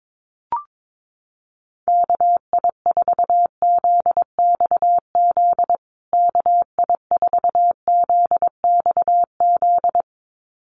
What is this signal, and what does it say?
Morse code, heard as a keyed tone, about 22 wpm, 700 Hz. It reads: KI4ZXZ KI4ZXZ
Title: USA Call Signs Repeated 2x 22wpm